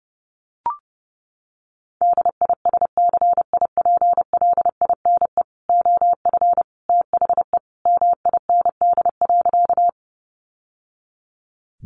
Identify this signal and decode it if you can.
Morse code, heard as a keyed tone, about 30 wpm, 700 Hz. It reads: DISCIPLINE OF THE MIND.
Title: Sentences from Top 900 Words 30wpm